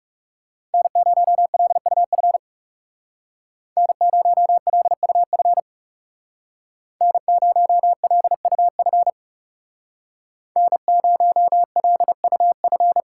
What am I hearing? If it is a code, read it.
Morse code, heard as a keyed tone, about 37 wpm, 700 Hz. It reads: N0LUF N0LUF N0LUF N0LUF
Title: Speed Racing - USA 1x3 Call Signs 30wpm